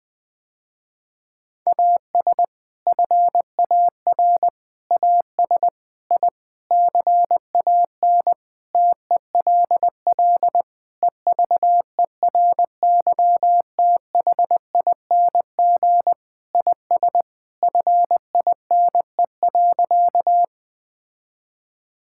Morse code, heard as a keyed tone, about 21 wpm, 700 Hz. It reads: AS FAR AS I CAN TELL EVERYTHING IS FINE.
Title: Sentences with 6-Word Phrases - Spoken Beginning - 20wpm